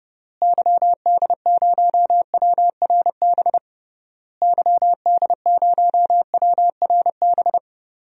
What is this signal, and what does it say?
Morse code, heard as a keyed tone, about 31 wpm, 700 Hz. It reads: YD0WRB YD0WRB
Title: DXCC-100 Call Signs - 6 Characters - Repeated 3x - 30wpm